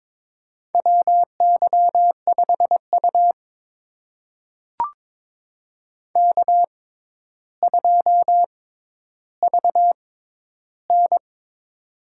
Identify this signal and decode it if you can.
Morse code, heard as a keyed tone, about 22 wpm, spaced as for 11 wpm, 700 Hz. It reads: WY5U K2VN
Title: USA Call Signs - 4 Characters - 6x Character Spacing - 22wpm